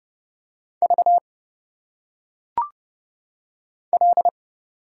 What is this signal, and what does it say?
Morse code, heard as a keyed tone, about 30 wpm, 700 Hz. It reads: V L
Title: Single Letters - Phonetic Pronunciation 30wpm